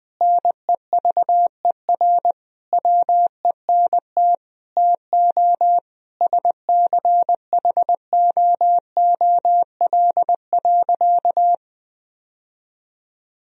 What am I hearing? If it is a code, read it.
Morse code, heard as a keyed tone, about 20 wpm, 700 Hz. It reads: NEVER WENT TO SCHOOL.
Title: Sentences from Top 1000 Words - Review Part 1 - 20wpm